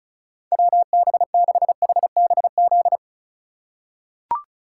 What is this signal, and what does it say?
Morse code, heard as a keyed tone, about 36 wpm, 700 Hz. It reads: WB6HBZ
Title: USA Call Signs - 6 Characters - 3x Character Spacing - 35wpm